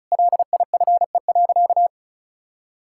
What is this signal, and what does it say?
Morse code, heard as a keyed tone, about 34 wpm, 700 Hz. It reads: LIFE.
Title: Sentences from Top 1000 Words - Review Part 6 - 35wpm